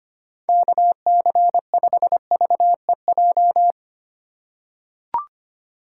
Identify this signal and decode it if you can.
Morse code, heard as a keyed tone, about 25 wpm, 700 Hz. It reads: KC5VEJ
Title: USA Call Signs - 6 Characters - 2x Character Spacing - 25wpm